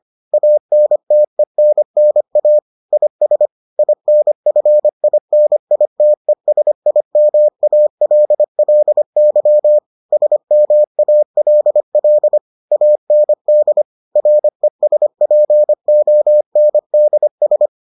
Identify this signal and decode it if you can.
Morse code, heard as a keyed tone, about 25 wpm, 600 Hz. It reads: ANTENNA IS INFINITESIMALLY SMALL AND RESPONDS